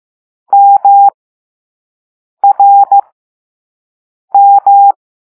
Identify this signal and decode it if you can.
Morse code, heard as a keyed tone, about 15 wpm, spaced as for 5 wpm, 800 Hz. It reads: MRM
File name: Morse / Level2B